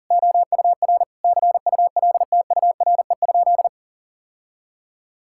Morse code, heard as a keyed tone, about 37 wpm, 700 Hz. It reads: OUR CULTURE?
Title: Sentences from Top 1000 Words - Review Part 6 - 40wpm